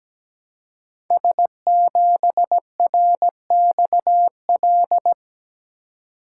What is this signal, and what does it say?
Morse code, heard as a keyed tone, about 17 wpm, 700 Hz. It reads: S7RXL
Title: Code Groups - 5 Letter:Numbers - 17wpm